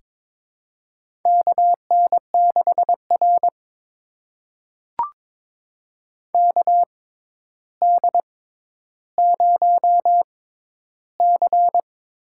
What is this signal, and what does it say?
Morse code, heard as a keyed tone, about 21 wpm, spaced as for 11 wpm, 700 Hz. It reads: KN6R KD0C
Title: USA Call Signs - 6x Character Spacing - 22wpm